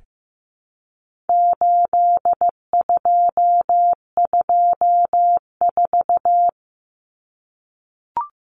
Morse code, heard as a keyed tone, about 15 wpm, 700 Hz. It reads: 8224
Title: Common Cut Numbers - 4-Digit 15wpm